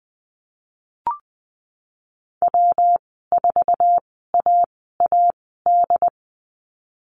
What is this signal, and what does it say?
Morse code, heard as a keyed tone, about 20 wpm, spaced as for 15 wpm, 700 Hz. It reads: W4AAD
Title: 173 Lesson 28g - Number 9 - Call Signs 20wpm